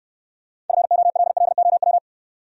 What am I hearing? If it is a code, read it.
Morse code, heard as a keyed tone, about 69 wpm, 690 Hz. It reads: FOLLOW